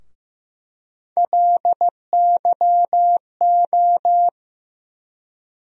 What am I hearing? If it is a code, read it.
Morse code, heard as a keyed tone, about 15 wpm, 700 Hz. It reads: LYO
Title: USA Call Sign Postfixes - 3 Letters 15wpm